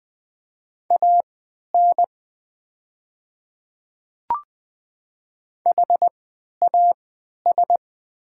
Morse code, heard as a keyed tone, about 20 wpm, spaced as for 11 wpm, 700 Hz. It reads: AN HAS
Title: 3x Character Spacing - Top 100 Words - 20wpm